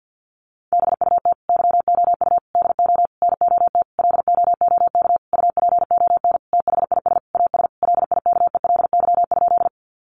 Morse code, heard as a keyed tone, about 50 wpm, 700 Hz. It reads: BUT YOU DO NOT LOOK UPON THIS AS LIKELY?
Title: Sentences from Top 1100 Words 50wpm